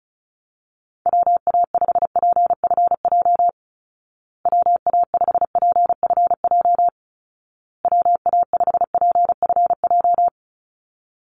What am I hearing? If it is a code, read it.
Morse code, heard as a keyed tone, about 34 wpm, 700 Hz. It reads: WA5PFJ WA5PFJ WA5PFJ
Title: Call Signs Repeated 2x - 35wpm